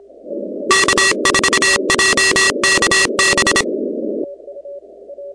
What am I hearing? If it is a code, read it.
Morse code, heard as a keyed tone, about 26 wpm, 1900 Hz, with noise. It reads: K4JKB
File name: contests / 2019-cw-wpx-kt5j-k5pi- -2330 0-17